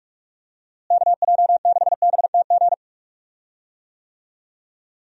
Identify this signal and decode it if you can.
Morse code, heard as a keyed tone, about 46 wpm, 700 Hz. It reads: KJ6BTG
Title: USA Call Signs - 6 Characters - Repeated 3x - 45wpm